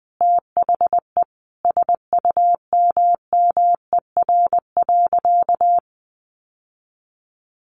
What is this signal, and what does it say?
Morse code, heard as a keyed tone, about 20 wpm, 700 Hz. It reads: THE SUMMER.
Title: Sentences from Top 1100 Words 20wpm